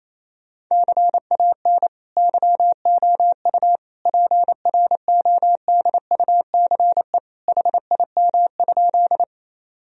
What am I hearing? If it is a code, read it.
Morse code, heard as a keyed tone, about 28 wpm, 700 Hz. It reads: CAN YOU PRODUCE HIM?
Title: Sentences from Top 1000 Words - Review Part 3 - 28wpm